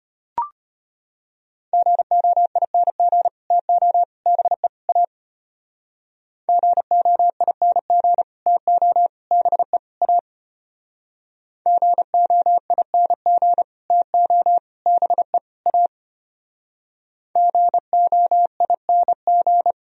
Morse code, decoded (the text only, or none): GOING TO BE A GOING TO BE A GOING TO BE A GOING